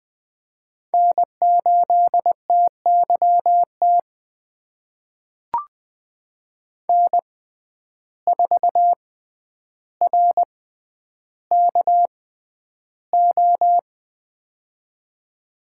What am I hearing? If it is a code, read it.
Morse code, heard as a keyed tone, about 20 wpm, spaced as for 10 wpm, 700 Hz. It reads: N8TYT N4RKO
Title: USA Call Signs - 5 Characters - 6x Character Spacing - 20wpm